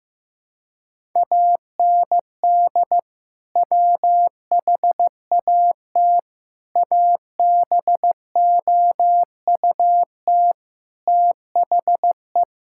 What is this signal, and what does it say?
Morse code, heard as a keyed tone, about 15 wpm, 700 Hz. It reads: AND WHAT ABOUT THE
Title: Sentences from Top 1000 Words - Review Part 5 - 15wpm